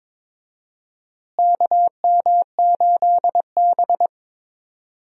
Morse code, heard as a keyed tone, about 21 wpm, 700 Hz. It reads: KM8B
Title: USA Call Signs - 4 Characters - Repeated 2x - 22wpm